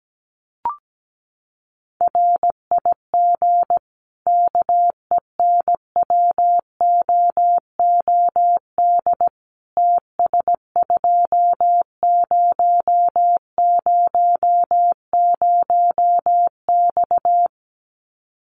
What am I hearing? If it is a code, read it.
Morse code, heard as a keyed tone, about 17 wpm, 700 Hz. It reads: RIG KENWOOD TS2000X
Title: Common QSO Phrases 17wpm